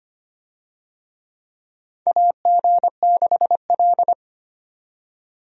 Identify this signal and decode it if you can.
Morse code, heard as a keyed tone, about 25 wpm, 700 Hz. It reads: AG6L
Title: USA Call Signs - 4 Characters - 25wpm